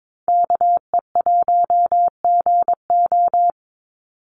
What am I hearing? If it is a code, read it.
Morse code, heard as a keyed tone, about 22 wpm, 700 Hz. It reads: KE1GO
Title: USA Call Signs 22wpm